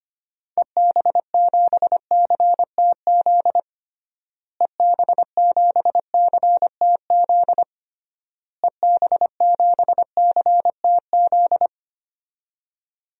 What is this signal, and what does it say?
Morse code, heard as a keyed tone, about 25 wpm, 700 Hz. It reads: EB7CTZ EB7CTZ EB7CTZ
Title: DXCC-100 Call Signs - 6 Characters - Repeated 3x - 25wpm